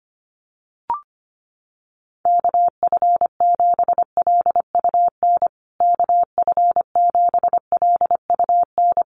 Morse code, heard as a keyed tone, about 25 wpm, 700 Hz. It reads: KF7LUN KF7LUN
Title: USA Call Signs Repeated 3x 25wpm